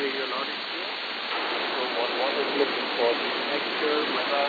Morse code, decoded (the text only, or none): none